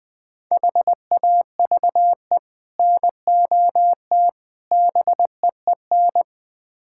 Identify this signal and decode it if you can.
Morse code, heard as a keyed tone, about 20 wpm, 700 Hz. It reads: HAVE NOT BEEN
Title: Build A Phrase - 5-Word Phrases - 20wpm